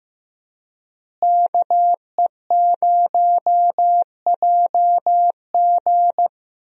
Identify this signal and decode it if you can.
Morse code, heard as a keyed tone, about 16 wpm, 700 Hz. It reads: KE0JG
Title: USA Call Signs - 5 Characters - Repeated 3x - 15wpm